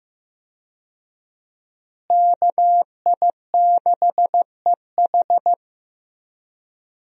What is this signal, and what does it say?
Morse code, heard as a keyed tone, about 16 wpm, 700 Hz. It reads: KI6EH
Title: USA Call Signs - 5 Characters - Repeated 3x - 15wpm